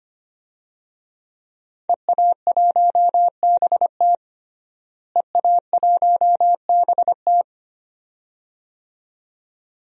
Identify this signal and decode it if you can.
Morse code, heard as a keyed tone, about 25 wpm, 700 Hz. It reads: EA1BT EA1BT
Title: DXCC-100 Call Signs - 5 Characters - Repeated 2x - 25wpm